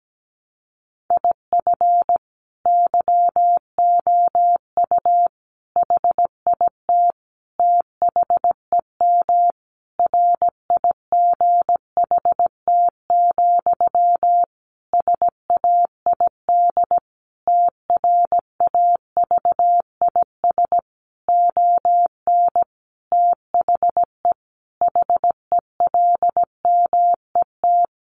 Morse code, heard as a keyed tone, about 17 wpm, 700 Hz. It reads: IF YOU HIT THEM RIGHT, SAID TRAVIS ON THE HELMET
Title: A Sound Of Thunder 17wpm